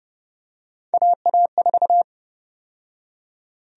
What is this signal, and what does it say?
Morse code, heard as a keyed tone, about 31 wpm, 700 Hz. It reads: AA4
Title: USA Call Sign Prefixes - 30wpm